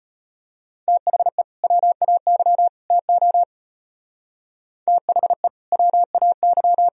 Morse code, heard as a keyed tone, about 32 wpm, 700 Hz. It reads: THE WAY TO THE WAY